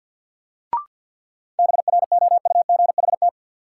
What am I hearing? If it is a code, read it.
Morse code, heard as a keyed tone, about 51 wpm, 700 Hz. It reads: BROUGHT
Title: Sentences from Top 500 Words - New Word Drill - 50wpm